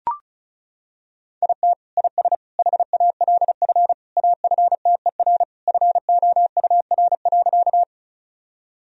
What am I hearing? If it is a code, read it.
Morse code, heard as a keyed tone, about 34 wpm, 700 Hz. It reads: IT IS HALF AFTER FOUR.
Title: Sentences from Top 1000 Words - Review Part 1 - 35wpm